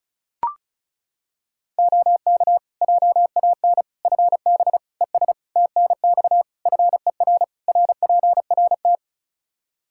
Morse code, heard as a keyed tone, about 34 wpm, 700 Hz. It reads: OK JAN FB ES TNX FER RPRT
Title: K1USN SST QSO Practice 35wpm